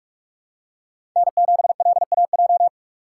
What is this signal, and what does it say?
Morse code, heard as a keyed tone, about 44 wpm, 700 Hz. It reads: N7LAJ
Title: USA Call Signs - 5 Characters - 4x Character Spacing - 45wpm